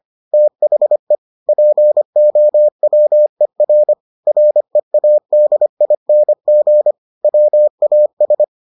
Morse code, heard as a keyed tone, about 25 wpm, 600 Hz. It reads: THE POWER READING WAS